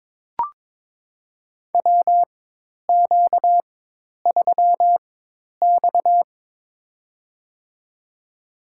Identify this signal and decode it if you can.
Morse code, heard as a keyed tone, about 22 wpm, spaced as for 10 wpm, 700 Hz. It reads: WQ3X
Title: USA Call Signs - 4 Characters - 4x Character Spacing - 22wpm